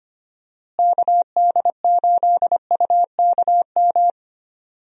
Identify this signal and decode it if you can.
Morse code, heard as a keyed tone, about 25 wpm, 700 Hz. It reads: KD8UKM
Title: USA Call Signs - 6 Characters - Repeated 2x - 25wpm